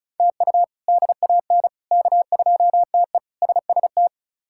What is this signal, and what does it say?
Morse code, heard as a keyed tone, about 34 wpm, 700 Hz. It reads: TU DAN K2TE SST
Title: K1USN SST QSO Practice - 35wpm